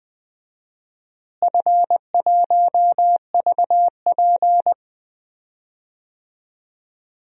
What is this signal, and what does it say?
Morse code, heard as a keyed tone, about 20 wpm, 700 Hz. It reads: F1VP
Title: DXCC-100 Call Signs - 4 Characters - Repeated 2x - 20wpm